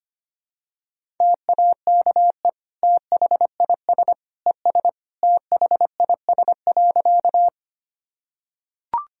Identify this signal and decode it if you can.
Morse code, heard as a keyed tone, about 25 wpm, 700 Hz. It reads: TAKE THIS ES THIS.
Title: Sentences from Top 200 Words and Abbreviations 25wpm